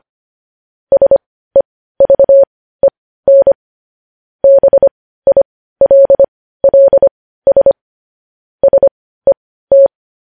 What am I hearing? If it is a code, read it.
Morse code, heard as a keyed tone, about 25 wpm, spaced as for 15 wpm, 550 Hz. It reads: SEVEN BILLS SET
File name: podcast / News headlines-1x-25x15